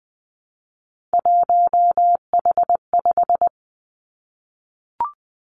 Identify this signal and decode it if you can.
Morse code, heard as a keyed tone, about 20 wpm, 700 Hz. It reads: 1H5
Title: Three Challenging Letter-Number Combinations 20wpm